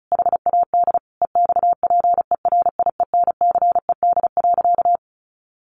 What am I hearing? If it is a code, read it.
Morse code, heard as a keyed tone, about 34 wpm, 700 Hz. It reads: HAD EXPERIENCED.
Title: Sentences from Top 1100 Words 35wpm